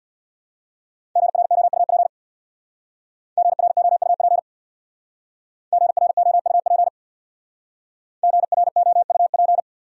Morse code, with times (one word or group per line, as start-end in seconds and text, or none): 1.2-2.1 GROUP
3.4-4.4 GROUP
5.7-6.9 GROUP
8.2-9.6 GROUP